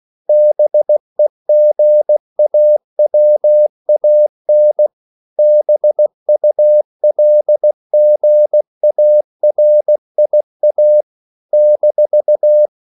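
Morse code, heard as a keyed tone, about 16 wpm, 600 Hz. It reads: BEGAWAN BULGARIA -
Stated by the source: CW practice of national capitals at 16 words per minute, from ON6ZQ.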